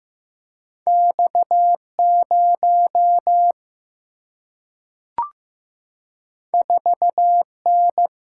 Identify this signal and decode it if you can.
Morse code, heard as a keyed tone, about 15 wpm, 700 Hz. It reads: X0 4N
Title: Single-Letter Single-Number 15wpm